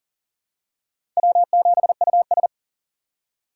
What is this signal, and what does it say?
Morse code, heard as a keyed tone, about 40 wpm, 700 Hz. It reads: W7US
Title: USA Call Signs - 4 Characters - 3x Character Spacing - 40wpm